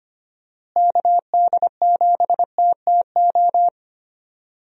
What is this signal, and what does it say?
Morse code, heard as a keyed tone, about 25 wpm, 700 Hz. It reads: KD7TTO
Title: USA Call Signs - 6 Characters - Repeated 3x - 25wpm